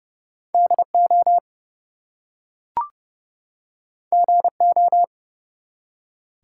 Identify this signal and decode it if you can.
Morse code, heard as a keyed tone, about 29 wpm, 700 Hz. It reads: DO GO
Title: Two Letter Words - 30wpm